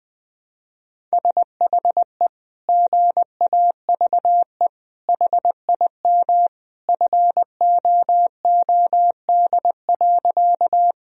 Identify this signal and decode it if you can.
Morse code, heard as a keyed tone, about 20 wpm, 700 Hz. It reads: SHE GAVE HIM FOOD.
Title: Sentences from Top 1000 Words - Review Part 1 - 20wpm